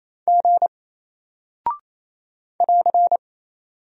Morse code, heard as a keyed tone, about 28 wpm, 700 Hz. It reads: G <AR>
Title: Single Character (Letter:Number:Punctuation:Prosign) Alt 28wpm